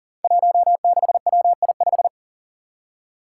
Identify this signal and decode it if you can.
Morse code, heard as a keyed tone, about 37 wpm, 700 Hz. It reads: 16WI5
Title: Five Challenging Letter-Number Combinations 40wpm